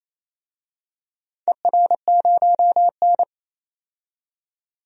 Morse code, heard as a keyed tone, about 27 wpm, 700 Hz. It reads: ER0N
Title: DXCC-100 Call Signs - 4 Characters - 28wpm